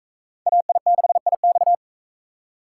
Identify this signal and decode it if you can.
Morse code, heard as a keyed tone, about 38 wpm, 700 Hz. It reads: AI6IX